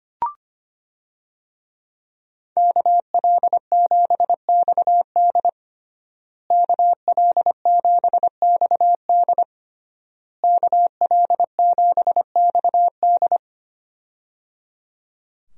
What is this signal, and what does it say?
Morse code, heard as a keyed tone, about 25 wpm, 700 Hz. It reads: KL7XD KL7XD KL7XD
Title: DXCC-100 Call Signs - 5 Characters - Repeated 3x - 25wpm